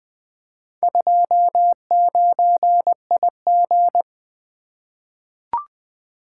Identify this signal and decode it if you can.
Morse code, heard as a keyed tone, about 20 wpm, 700 Hz. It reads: 29IG
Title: Four Letter-Number Combinations - 20wpm